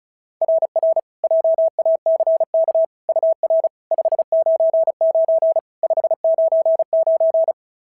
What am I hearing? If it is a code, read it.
Morse code, heard as a keyed tone, about 36 wpm, 690 Hz. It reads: RR JACK UR 599 599
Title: SOTA QSO Practice 35wpm